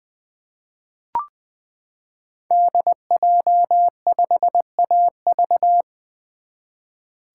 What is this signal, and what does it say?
Morse code, heard as a keyed tone, about 20 wpm, 700 Hz. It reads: DJ5AV
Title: 2023 SOTA Chaser Call Signs - 20wpm